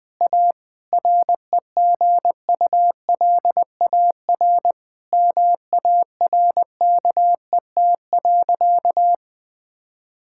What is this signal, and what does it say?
Morse code, heard as a keyed tone, about 20 wpm, 700 Hz. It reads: A REGULAR MARKET.
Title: Sentences from Top 1000 Words - Review Part 6 - 20wpm